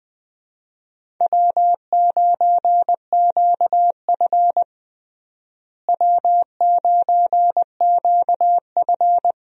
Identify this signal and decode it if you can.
Morse code, heard as a keyed tone, about 20 wpm, 700 Hz. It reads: W9QF W9QF
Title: USA Call Signs - 4 Characters - Repeated 3x - 20wpm